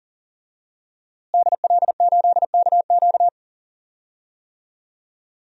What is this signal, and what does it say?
Morse code, heard as a keyed tone, about 37 wpm, 700 Hz. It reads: DL8KQ
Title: DXCC-100 Call Signs - 5 Characters - Repeated 3x - 40wpm